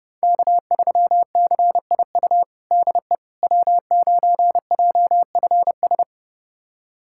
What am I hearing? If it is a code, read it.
Morse code, heard as a keyed tone, about 32 wpm, 700 Hz. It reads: K3CIU DE W9JFS
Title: USA Callsign DE Callsign - 5 Characters - 30wpm